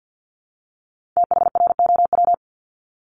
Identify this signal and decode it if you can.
Morse code, heard as a keyed tone, about 50 wpm, 700 Hz. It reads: THROW
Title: Multi-letter Phonemes 50wpm